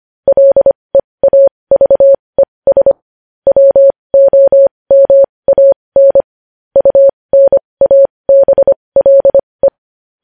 Morse code, heard as a keyed tone, about 25 wpm, spaced as for 20 wpm, 550 Hz. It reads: LEAVES WOMAN UNABLE
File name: podcast / News headlines-1x-25x20